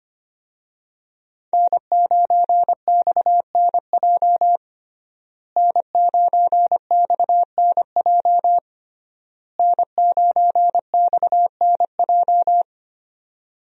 Morse code, heard as a keyed tone, about 25 wpm, 700 Hz. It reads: N9XNJ N9XNJ N9XNJ
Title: USA Call Signs - 5 Characters - Repeated 3x - 25wpm